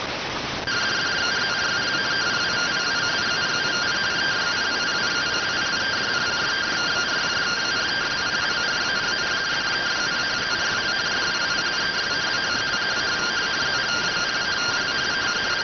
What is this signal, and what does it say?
Two-tone teleprinter signal, about 170 Hz shift.